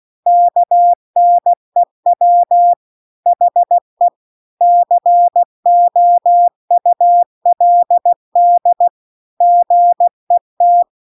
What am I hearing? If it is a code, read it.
Morse code, heard as a keyed tone, about 16 wpm, 700 Hz. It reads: KNEW HE COULD GET